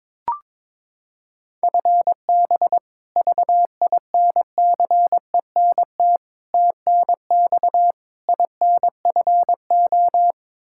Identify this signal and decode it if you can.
Morse code, heard as a keyed tone, about 22 wpm, 700 Hz. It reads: FB VINCENT TNX INFO
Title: Practice QSO - Exchange 3 - More Info - 22wpm